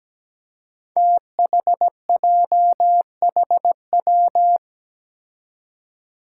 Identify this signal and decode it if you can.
Morse code, heard as a keyed tone, about 17 wpm, 700 Hz. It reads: THJHW
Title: Five Challenging Letter-Number Combinations 17wpm